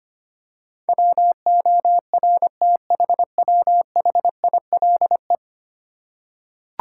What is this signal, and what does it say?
Morse code, heard as a keyed tone, about 25 wpm, 700 Hz. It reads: WORTHWHILE
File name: Ten Letter Words - 25wpm